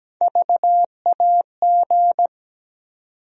Morse code, heard as a keyed tone, about 17 wpm, 700 Hz. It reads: VAG
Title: USA Call Sign Postfixes - 3 Letters 17wpm